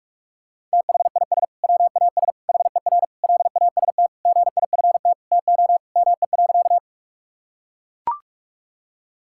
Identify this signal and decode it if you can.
Morse code, heard as a keyed tone, about 41 wpm, 700 Hz. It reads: THIS WAS HER LAST GIFT TO ME.
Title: Sentences from Top 1000 Words - Review Part 6 - 45wpm